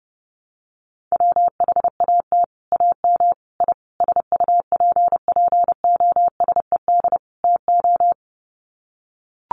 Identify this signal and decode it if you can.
Morse code, heard as a keyed tone, about 30 wpm, 700 Hz. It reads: WHAT AM I SUPPOSED TO
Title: Sets of 5 Words Encore2 30wpm